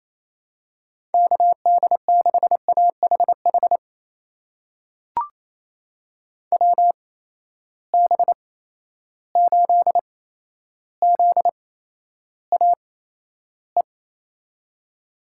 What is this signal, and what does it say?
Morse code, heard as a keyed tone, about 28 wpm, spaced as for 12 wpm, 700 Hz. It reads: KD6AHH WB8ZAE
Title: USA Call Signs - 6 Characters - 8x Character Spacing - 28wpm